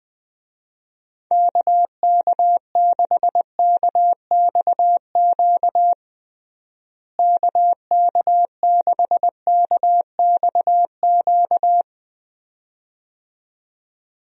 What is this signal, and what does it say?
Morse code, heard as a keyed tone, about 20 wpm, 700 Hz. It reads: KK6KXQ KK6KXQ
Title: USA Call Signs - 6 Characters - Repeated 2x - 20wpm